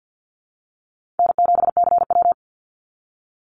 Morse code, heard as a keyed tone, about 50 wpm, 700 Hz. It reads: N7CW
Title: NA SOTA Chaser Callsigns 50wpm